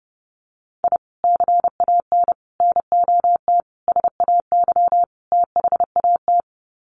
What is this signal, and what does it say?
Morse code, heard as a keyed tone, about 30 wpm, 700 Hz. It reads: I CAN NOT SAY THAT
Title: Sets of 5 Words Encore2 30wpm